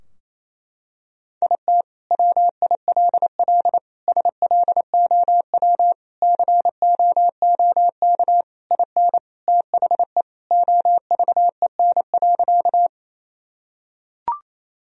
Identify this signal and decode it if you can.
Morse code, heard as a keyed tone, about 28 wpm, 700 Hz. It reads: IT WILL SLOW COOK IN THE OVEN.
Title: Sentences from Top 1000 Words - Review Part 4 - 28wpm